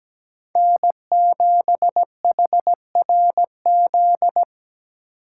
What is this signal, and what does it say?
Morse code, heard as a keyed tone, about 17 wpm, 700 Hz. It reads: N7HRZ
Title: USA Call Signs - 5 Characters - 8x Character Spacing - 17wpm